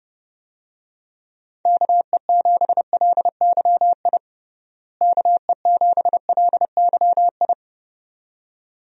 Morse code, heard as a keyed tone, about 32 wpm, 700 Hz. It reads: KE7LYI KE7LYI
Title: USA Call Signs - 6 Characters - Repeated 2x - 30wpm